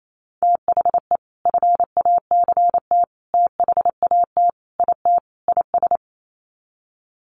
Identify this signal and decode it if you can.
Morse code, heard as a keyed tone, about 28 wpm, 700 Hz. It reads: THE FACT THAT IT IS
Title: Sets of 5 Words Encore2 28wpm